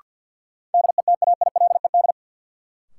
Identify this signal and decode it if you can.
Morse code, heard as a keyed tone, about 51 wpm, 700 Hz. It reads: DETAILED
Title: Sentences from Top 1500 Words - New Word Drill - 50wpm